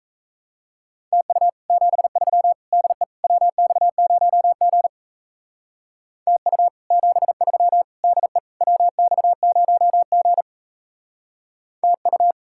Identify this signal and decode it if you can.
Morse code, heard as a keyed tone, about 36 wpm, 700 Hz. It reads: TU 73 DE WX0G TU 73 DE WX0G TU